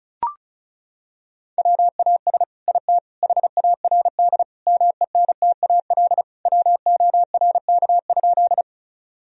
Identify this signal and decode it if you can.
Morse code, heard as a keyed tone, about 36 wpm, 700 Hz. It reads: WAS IT HARD MENTAL WORK?
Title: Sentences from Top 1000 Words - Review Part 4 - 35wpm